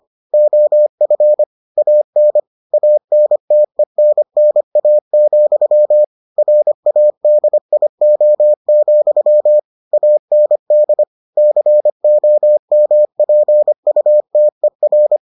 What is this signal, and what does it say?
Morse code, heard as a keyed tone, about 25 wpm, 600 Hz. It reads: OF AN ANTENNA, RADIO, AND COMPUTER